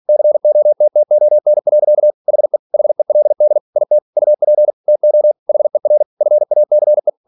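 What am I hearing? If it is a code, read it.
Morse code, heard as a keyed tone, about 46 wpm, 600 Hz. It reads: COTTON. HE HELD IT UP TO HER FACE
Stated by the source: Practice text from an English novel sent at 47 WPM.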